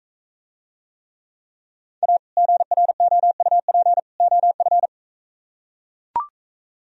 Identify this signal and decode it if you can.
Morse code, heard as a keyed tone, about 41 wpm, 700 Hz. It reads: A GROUP OF